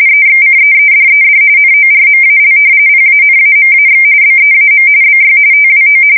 RTTY, about 45.45 baud. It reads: EXCEEDED NORMAL PANAMAX BEAM BY TWO
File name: fsk170 / battleships